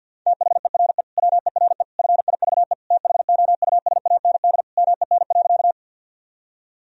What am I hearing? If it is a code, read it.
Morse code, heard as a keyed tone, about 52 wpm, 700 Hz. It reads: THERE WERE FIVE THOUSAND MEN.
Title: Sentences from Top 1000 Words - Review Part 3 - 50wpm